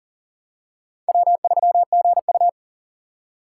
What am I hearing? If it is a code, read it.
Morse code, heard as a keyed tone, about 40 wpm, 700 Hz. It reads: W3GU
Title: USA Call Signs - 4 Characters - 6x Character Spacing - 40wpm